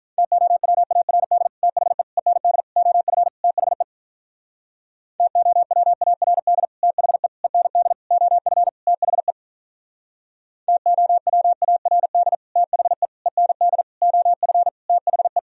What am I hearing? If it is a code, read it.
Morse code, heard as a keyed tone, about 43 wpm, 700 Hz. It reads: TOWARD THE END OF THE TOWARD THE END OF THE TOWARD THE END OF THE
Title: Speed-Racing - Sets of 5 Words 35wpm